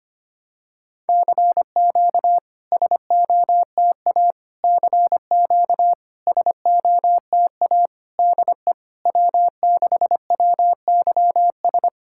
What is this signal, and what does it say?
Morse code, heard as a keyed tone, about 25 wpm, 700 Hz. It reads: CQ SOTA CQ SOTA DE W6WYS
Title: SOTA QSO Practice 25wpm